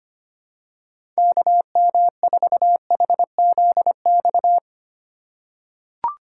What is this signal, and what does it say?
Morse code, heard as a keyed tone, about 25 wpm, 700 Hz. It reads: KM4HZX
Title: USA Call Signs - 6 Characters - 2x Character Spacing - 25wpm